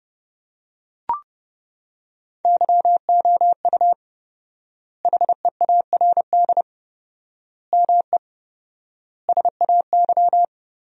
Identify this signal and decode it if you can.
Morse code, heard as a keyed tone, about 30 wpm, 700 Hz. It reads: YOU HEARD ME SAY
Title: 4x Word-Spacing - Sentences from Top 500 Words - 30wpm